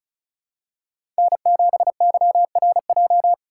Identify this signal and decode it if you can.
Morse code, heard as a keyed tone, about 36 wpm, 700 Hz. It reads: N7YRJ
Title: USA Call Signs - 5 Characters - 3x Character Spacing - 35wpm